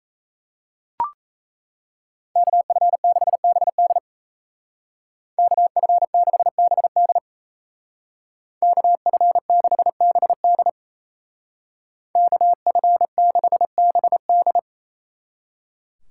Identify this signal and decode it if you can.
Morse code, heard as a keyed tone, about 37 wpm, 700 Hz. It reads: KF6BD KF6BD KF6BD KF6BD